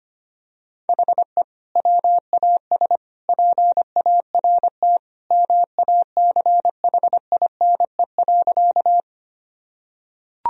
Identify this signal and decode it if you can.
Morse code, heard as a keyed tone, about 25 wpm, 700 Hz. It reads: HE WAS PART MACHINE.
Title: Sentences from Top 1000 Words - Review Part 2 - 25wpm